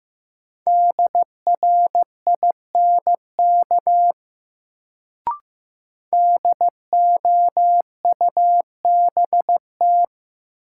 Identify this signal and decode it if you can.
Morse code, heard as a keyed tone, about 15 wpm, 700 Hz. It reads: DRINK DOUBT
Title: Sentences from Top 1500 Words - New Word Drill Alt - 15wpm